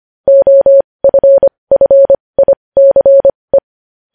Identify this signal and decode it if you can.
Morse code, heard as a keyed tone, about 25 wpm, spaced as for 20 wpm, 550 Hz. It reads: OFFICE
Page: news-headlines-in-morse-code-at-20-wpm